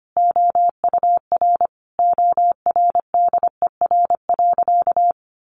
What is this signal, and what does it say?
Morse code, heard as a keyed tone, about 25 wpm, 700 Hz. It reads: OUR ORDER.
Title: Sentences from Top 1300 Words 25wpm